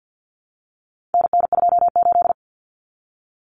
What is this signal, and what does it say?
Morse code, heard as a keyed tone, about 50 wpm, 700 Hz. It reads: NN28
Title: Common Cut Numbers - 4-Digit 50wpm